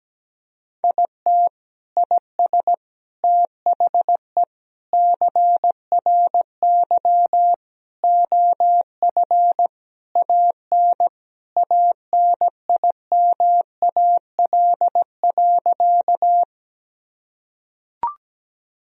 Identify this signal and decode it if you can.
Morse code, heard as a keyed tone, about 17 wpm, 700 Hz. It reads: IT IS THE CRY OF AN ANIMAL.
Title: Sentences from Top 1000 Words - Review Part 2 - 17wpm